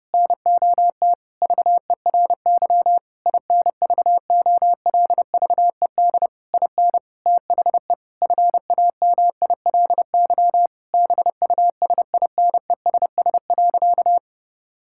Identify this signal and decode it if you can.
Morse code, heard as a keyed tone, about 30 wpm, 700 Hz. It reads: NOT VERY INVOLVED IN THE FAMILY BUSINESS.
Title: Sentences from Top 1000 Words - Review Part 5 - 30wpm